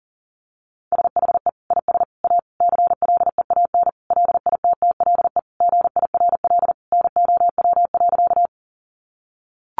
Morse code, heard as a keyed tone, about 40 wpm, 700 Hz. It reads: SHE IS A CLEAN LITTLE GIRL NOW.
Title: Sentences from Top 1100 Words 40wpm